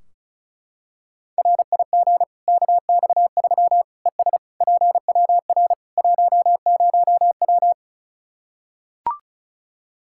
Morse code, heard as a keyed tone, about 36 wpm, 700 Hz. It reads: RIG KX3 ES PWR 10W
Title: Practice QSO - Exchange 2 - Info - 35wpm